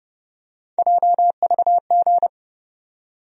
Morse code, heard as a keyed tone, about 30 wpm, 700 Hz. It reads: JVG
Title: USA Call Sign Postfixes - 3 Letters 30wpm